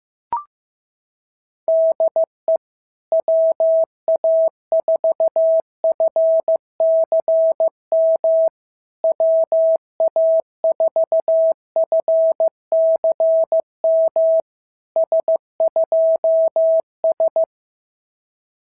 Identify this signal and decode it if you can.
Morse code, heard as a keyed tone, about 15 wpm, 650 Hz. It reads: DE WA4FCM WA4FCM S2S
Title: SOTA QSO Practice 15wpm